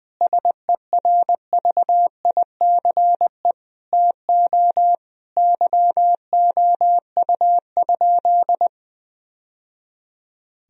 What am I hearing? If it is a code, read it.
Morse code, heard as a keyed tone, about 20 wpm, 700 Hz. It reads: SERVICE TO YOU?
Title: Sentences from Top 1000 Words - Review Part 4 - 20wpm